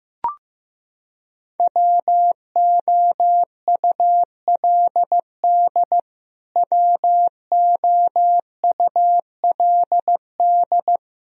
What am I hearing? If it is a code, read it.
Morse code, heard as a keyed tone, about 15 wpm, 700 Hz. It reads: WOULD WOULD
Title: Top 100 Words Repeated 15wpm